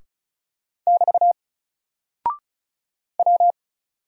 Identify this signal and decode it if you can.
Morse code, heard as a keyed tone, about 33 wpm, 700 Hz. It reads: <BT> W
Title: Single Character (Letter:Number:Punctuation:Prosign) 35wpm